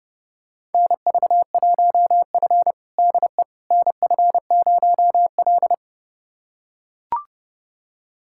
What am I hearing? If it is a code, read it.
Morse code, heard as a keyed tone, about 30 wpm, 700 Hz. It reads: NV1F DE NF0L
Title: USA Callsign DE Callsign - 4 Characters - 30wpm